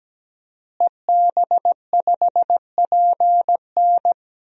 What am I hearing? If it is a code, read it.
Morse code, heard as a keyed tone, about 17 wpm, 700 Hz. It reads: EB5PN
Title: DXCC-100 Call Signs - 5 Characters - Repeated 2x - 17wpm